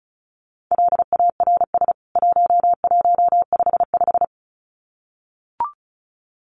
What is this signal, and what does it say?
Morse code, heard as a keyed tone, about 34 wpm, 700 Hz. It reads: LARS 1155
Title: CWT Exchange 35wpm